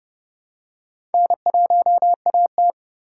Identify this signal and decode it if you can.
Morse code, heard as a keyed tone, about 31 wpm, 700 Hz. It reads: N1AT